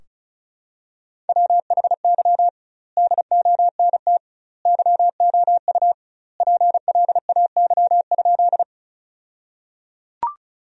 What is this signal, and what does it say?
Morse code, heard as a keyed tone, about 34 wpm, 700 Hz. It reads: WHY DONT YOU PLAY?
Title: 2x Word-Spacing - Sentences from Top 300 Words - 35wpm